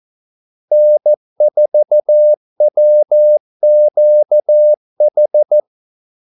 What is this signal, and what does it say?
Morse code, heard as a keyed tone, about 14 wpm, 600 Hz. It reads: N4WQH
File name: Callsigns-14wpm